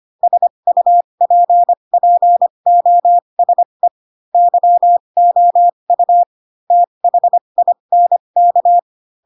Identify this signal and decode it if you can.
Morse code, heard as a keyed tone, about 25 wpm, spaced as for 22 wpm, 700 Hz. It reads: SUPPOSE YOU THINK
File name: Pinocchio / Pinocchio ch2 18 25 ik1hge